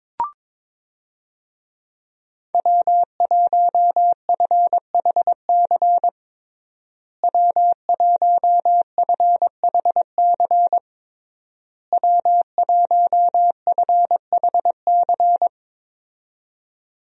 Morse code, heard as a keyed tone, about 22 wpm, 700 Hz. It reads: W1FHC W1FHC W1FHC
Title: USA Call Signs - 5 Characters - Repeated 3x - 22wpm